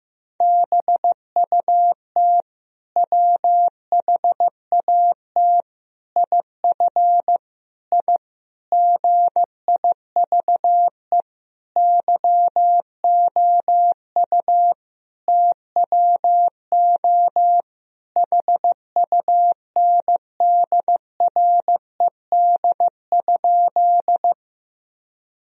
Morse code, heard as a keyed tone, about 15 wpm, 700 Hz. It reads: BUT WHAT IF I GIVE YOU TWO HUNDRED?
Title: Sentences from Top 1000 Words - Review Part 2 - 15wpm